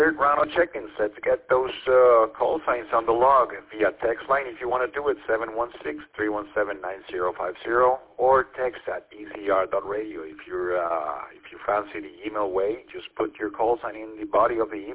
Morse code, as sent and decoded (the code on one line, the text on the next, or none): none